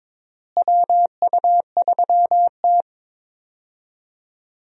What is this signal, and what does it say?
Morse code, heard as a keyed tone, about 21 wpm, 700 Hz. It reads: WU3T
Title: USA Call Signs - 4 Characters - Repeated 2x - 22wpm